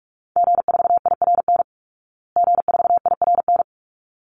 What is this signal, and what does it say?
Morse code, heard as a keyed tone, about 44 wpm, 700 Hz. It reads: G4IRN G4IRN
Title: Call Signs Repeated 3x - 45wpm